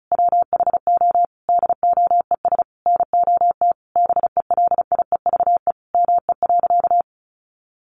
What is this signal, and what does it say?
Morse code, heard as a keyed tone, about 34 wpm, 700 Hz. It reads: WHO DOES NOT BELIEVE ME.
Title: Sentences from Top 1000 Words 35wpm